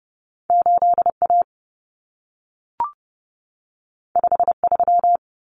Morse code, heard as a keyed tone, about 30 wpm, 700 Hz. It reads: 8A 53
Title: Common Cut Numbers - 2-Digit 30wpm